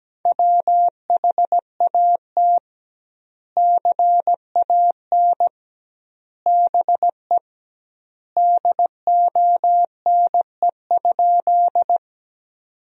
Morse code, heard as a keyed tone, about 17 wpm, 700 Hz. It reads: WHAT CAN BE DONE?
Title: 2x Word-Spacing - Sentences from Top 500 Words - 17wpm